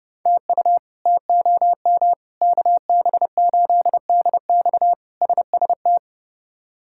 Morse code, heard as a keyed tone, about 30 wpm, 700 Hz. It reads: TU TOM KB8DX SST
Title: K1USN SST QSO Practice - 30wpm